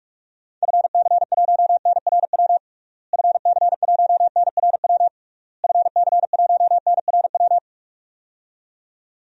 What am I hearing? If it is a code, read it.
Morse code, heard as a keyed tone, about 46 wpm, 700 Hz. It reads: FC1NRW FC1NRW FC1NRW
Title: DXCC-100 Call Signs - 6 Characters - Repeated 3x - 45wpm